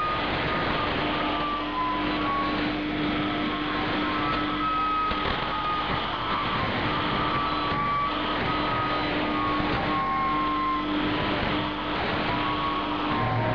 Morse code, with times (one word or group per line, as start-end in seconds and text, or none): none